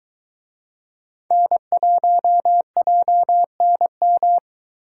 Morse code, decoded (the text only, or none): N1JNM